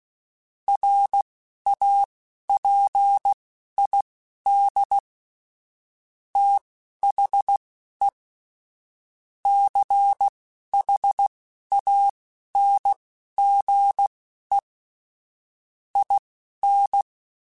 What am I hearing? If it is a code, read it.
Morse code, heard as a keyed tone, about 16 wpm, spaced as for 11 wpm, 800 Hz. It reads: RAPID THE CHANGE IN